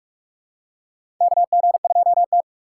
Morse code, heard as a keyed tone, about 46 wpm, 700 Hz. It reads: KG2T
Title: USA Call Signs - 4 Characters - 6x Character Spacing - 45wpm